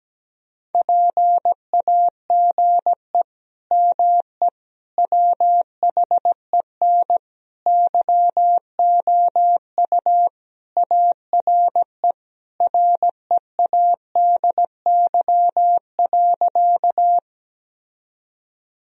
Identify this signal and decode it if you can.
Morse code, heard as a keyed tone, about 17 wpm, 700 Hz. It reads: PAGE ME WHEN YOU ARE READY.
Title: Sentences from Top 1000 Words - Review Part 1 - 17wpm